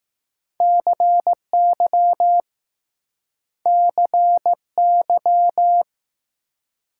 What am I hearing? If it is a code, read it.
Morse code, heard as a keyed tone, about 20 wpm, 700 Hz. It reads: CY CY